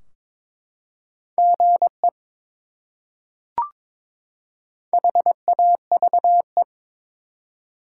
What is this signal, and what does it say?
Morse code, heard as a keyed tone, about 23 wpm, 700 Hz. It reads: GE HAVE
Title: Top 180 QSO Elements 22wpm